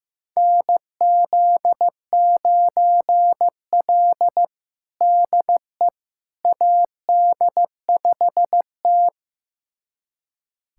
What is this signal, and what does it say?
Morse code, heard as a keyed tone, about 15 wpm, 700 Hz. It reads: NZ9L DE AD5T
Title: USA Callsign DE Callsign - 4 Characters - 15wpm